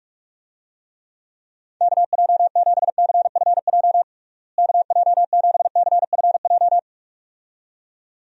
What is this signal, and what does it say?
Morse code, heard as a keyed tone, about 46 wpm, 700 Hz. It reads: KJ7CFJ KJ7CFJ
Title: USA Call Signs - 6 Characters - Repeated 2x - 45wpm